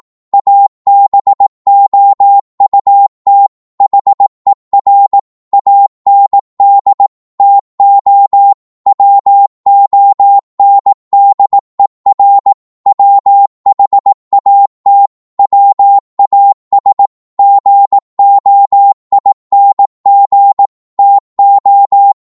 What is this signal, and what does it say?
Morse code, heard as a keyed tone, about 18 wpm, 800 Hz. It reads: ABOUT HER AND TO WONDER WHAT WAS GOING TO
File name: Morse / psg-18wpm